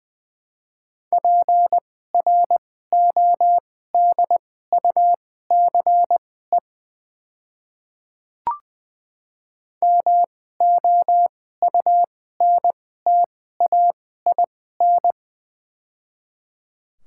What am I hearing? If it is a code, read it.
Morse code, heard as a keyed tone, about 21 wpm, spaced as for 15 wpm, 700 Hz. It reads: PRODUCE MOUNTAIN
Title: 2x Character Spacing - Top 500 Words - 20wpm